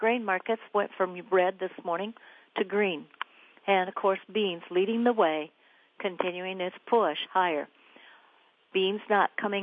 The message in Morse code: none

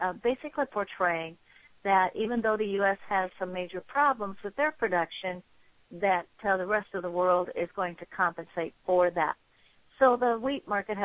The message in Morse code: none

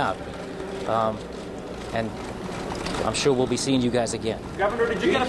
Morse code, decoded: none

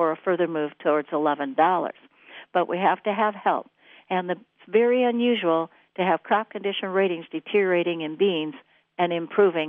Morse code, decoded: none

